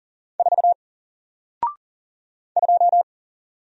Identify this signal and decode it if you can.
Morse code, heard as a keyed tone, about 40 wpm, 700 Hz. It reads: 4 2
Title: Single Character (Letter:Number:Punctuation:Prosign) Alt 40wpm